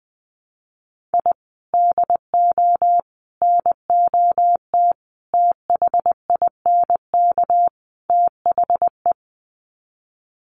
Morse code, heard as a keyed tone, about 20 wpm, 700 Hz. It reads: I DO NOT THINK THE
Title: Sets of 5 Words 20wpm